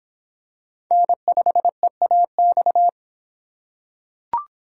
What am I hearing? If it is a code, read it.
Morse code, heard as a keyed tone, about 24 wpm, 700 Hz. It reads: N5EAX